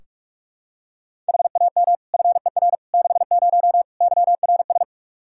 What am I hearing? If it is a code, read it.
Morse code, heard as a keyed tone, about 44 wpm, 700 Hz. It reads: HAM FER 60 YRS
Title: K1USN SST QSO Practice 45wpm